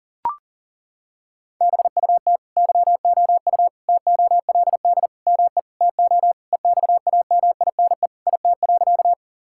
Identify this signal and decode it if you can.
Morse code, heard as a keyed tone, about 37 wpm, 700 Hz. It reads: BUT YOU TOLD ME TO EXAMINE IT.
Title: Sentences from Top 1600 Words 40wpm